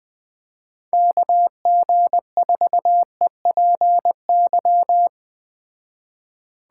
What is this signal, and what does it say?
Morse code, heard as a keyed tone, about 20 wpm, 700 Hz. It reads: KG4EPY
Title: Speed Racing - USA 2x3 Call Signs 17wpm